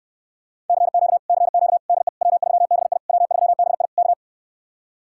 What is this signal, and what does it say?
Morse code, heard as a keyed tone, about 69 wpm, 700 Hz. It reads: CQ CQ DE W3BI W3BI K